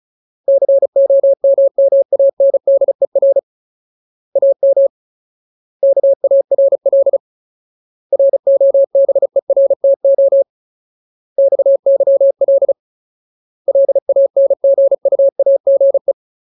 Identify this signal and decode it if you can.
Morse code, heard as a keyed tone, about 34 wpm, 550 Hz. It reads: COMMANDER AM KARL ROBERTO XYL LANGUAGE
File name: Ultimate Word List / 35WPM Ultimate